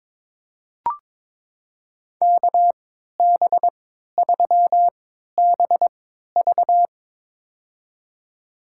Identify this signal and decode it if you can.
Morse code, heard as a keyed tone, about 22 wpm, spaced as for 12 wpm, 700 Hz. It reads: KB3BV
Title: USA Call Signs - 5 Characters - 3x Character Spacing - 22wpm